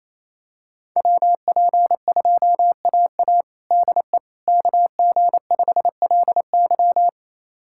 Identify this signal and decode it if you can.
Morse code, heard as a keyed tone, about 28 wpm, 700 Hz. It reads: WP2AA DE KG5LY
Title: USA Callsign DE Callsign - 5 Characters - 28wpm